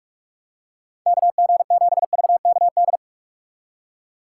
Morse code, heard as a keyed tone, about 43 wpm, 700 Hz. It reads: KG7VKD
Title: USA Call Signs - 6 Characters - 3x Character Spacing - 45wpm